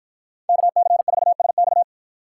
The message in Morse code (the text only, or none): KC4SX